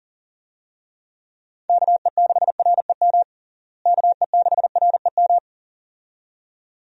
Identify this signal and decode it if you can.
Morse code, heard as a keyed tone, about 37 wpm, 700 Hz. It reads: KE6REM KE6REM
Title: USA Call Signs - 6 Characters - Repeated 2x - 40wpm